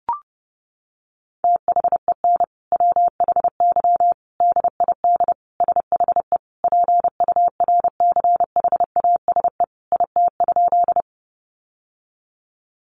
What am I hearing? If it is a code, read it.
Morse code, heard as a keyed tone, about 30 wpm, 700 Hz. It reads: THEN WHY DID SHE PURCHASE IT?
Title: Sentences from Top 800 Words 30wpm